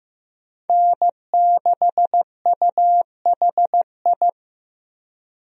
Morse code, heard as a keyed tone, about 15 wpm, 700 Hz. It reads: N6UHI
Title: USA Call Signs - 5 Characters - 15wpm